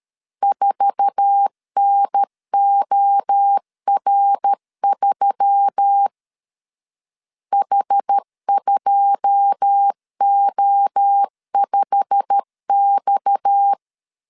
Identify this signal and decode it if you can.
Morse code, heard as a keyed tone, about 13 wpm, 800 Hz. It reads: 4NOR3 H2O5X
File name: CW / DET / 1 9 1500C S A 12wpm 3800